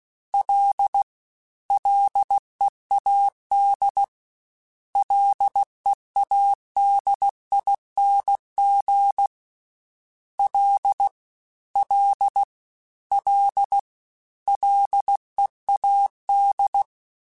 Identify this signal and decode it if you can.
Morse code, heard as a keyed tone, about 16 wpm, 800 Hz. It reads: L LEAD LEADING L L L LEAD
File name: Combinations / Combinations-mp3s / Combinations@15